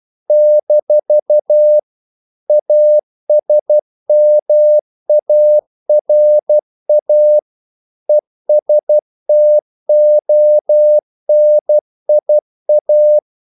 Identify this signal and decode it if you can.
Morse code, heard as a keyed tone, about 12 wpm, 600 Hz. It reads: - ASMARA ESTONIA
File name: CW / Practice / NationalCapitals12